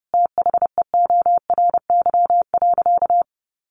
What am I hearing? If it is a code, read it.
Morse code, heard as a keyed tone, about 30 wpm, 700 Hz. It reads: THEORY.
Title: Sentences from Top 800 Words 30wpm